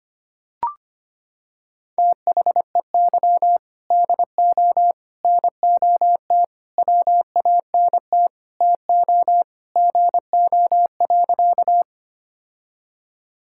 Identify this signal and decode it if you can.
Morse code, heard as a keyed tone, about 25 wpm, 700 Hz. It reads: THEY DO NOT WANT TO GO.
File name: practice / 25wpm / Sentences with 3-Word Phrases - 25wpm